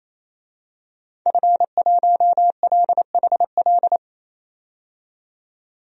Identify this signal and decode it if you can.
Morse code, heard as a keyed tone, about 28 wpm, 700 Hz. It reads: F1LHL
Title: DXCC-100 Call Signs - 5 Characters - 28wpm